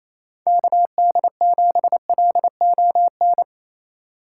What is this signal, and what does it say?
Morse code, heard as a keyed tone, about 27 wpm, 700 Hz. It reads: KD7LON
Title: USA Call Signs - 4x Character Spacing - 28wpm